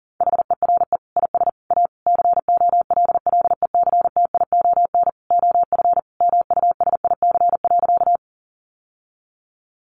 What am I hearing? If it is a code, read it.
Morse code, heard as a keyed tone, about 40 wpm, 700 Hz. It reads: HERE IS A COLLECTION OF MUSIC.
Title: Sentences from Top 1200 Words 40wpm